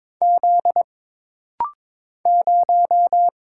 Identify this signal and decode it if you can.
Morse code, heard as a keyed tone, about 22 wpm, 700 Hz. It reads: Z 0
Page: This rapid-fire timing is geared to push your ICR (Instant Character Recognition) skills.